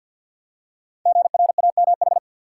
Morse code, heard as a keyed tone, about 46 wpm, 700 Hz. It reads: GRAMH
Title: Code Groups - 5 Letters - 50wpm